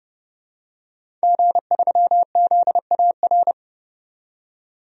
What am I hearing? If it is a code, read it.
Morse code, heard as a keyed tone, about 32 wpm, 700 Hz. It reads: G3ZAR
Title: DXCC-100 Call Signs - 5 Characters - 30wpm